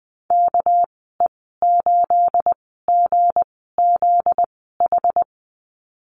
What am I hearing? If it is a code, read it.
Morse code, heard as a keyed tone, about 20 wpm, spaced as for 14 wpm, 700 Hz. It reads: KE8GZH
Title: 245 Lesson 39g - Letter z - Call Signs 20wpm